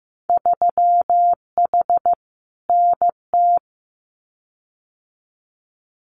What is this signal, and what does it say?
Morse code, heard as a keyed tone, about 15 wpm, 700 Hz. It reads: 3H NT
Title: Winter Field Day Exchanges Encore 15wpm